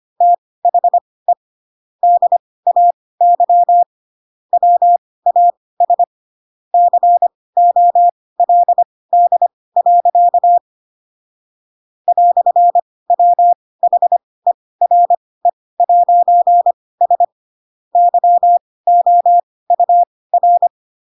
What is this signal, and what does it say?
Morse code, heard as a keyed tone, about 25 wpm, spaced as for 18 wpm, 700 Hz. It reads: THE DAY WAS COLD. "WHERE'S YOUR